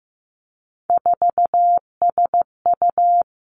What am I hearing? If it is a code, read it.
Morse code, heard as a keyed tone, about 15 wpm, 700 Hz. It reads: 4SU
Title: Three Challenging Letter-Number Combinations 15wpm